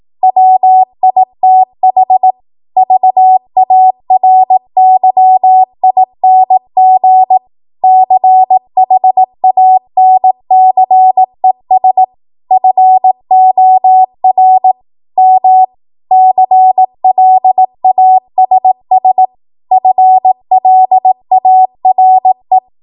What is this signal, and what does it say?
Morse code, heard as a keyed tone, about 18 wpm, 750 Hz. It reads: WITH VARYING CHANCES FOR M CLASS FLARE
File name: Morse / 18WPM